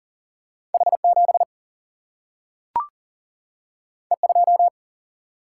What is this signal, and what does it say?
Morse code, heard as a keyed tone, about 40 wpm, 700 Hz. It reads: H 7 E 2
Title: Single-Letter Single-Number 40wpm